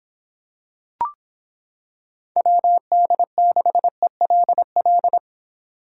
Morse code, heard as a keyed tone, about 26 wpm, 700 Hz. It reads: WD6ELL